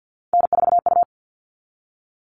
Morse code, heard as a keyed tone, about 50 wpm, 700 Hz. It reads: N4U
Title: All Cut Numbers - 3-Digit 50wpm